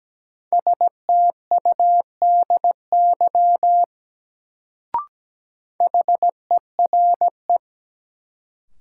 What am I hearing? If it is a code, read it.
Morse code, heard as a keyed tone, about 17 wpm, 700 Hz. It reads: STUDY HERE
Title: Sets of 5 Words - New Word Drill - 17wpm